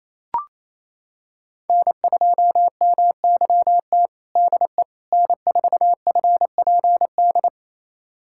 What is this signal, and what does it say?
Morse code, heard as a keyed tone, about 28 wpm, 700 Hz. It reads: N2MYT DE N4FPD
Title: USA Callsign DE Callsign - 5 Characters - 28wpm